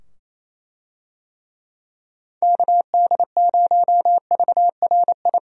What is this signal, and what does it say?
Morse code, heard as a keyed tone, about 28 wpm, 700 Hz. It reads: KD0VRI
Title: USA Call Signs - 6 Characters - 28wpm